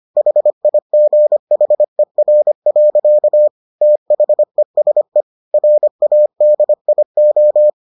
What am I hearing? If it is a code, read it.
Morse code, heard as a keyed tone, about 25 wpm, 600 Hz. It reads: HIGHER. THESE RADIO